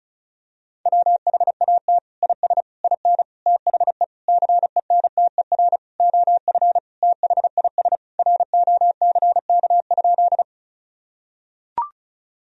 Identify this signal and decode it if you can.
Morse code, heard as a keyed tone, about 36 wpm, 700 Hz. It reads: WHAT IS IN THE CENTER OF THIS ROCK?
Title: Sentences from Top 1000 Words - Review Part 2 - 35wpm